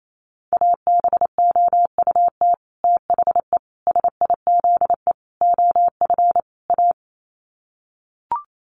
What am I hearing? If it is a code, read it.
Morse code, heard as a keyed tone, about 28 wpm, 700 Hz. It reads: ABOUT THE SIZE OF A
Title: Sets of 5 Words 28wpm